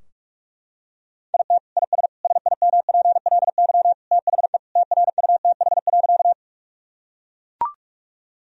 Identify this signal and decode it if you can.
Morse code, heard as a keyed tone, about 41 wpm, 700 Hz. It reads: IT IS SIMPLY THE TRUTH.
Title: Sentences from Top 1000 Words - Review Part 4 - 45wpm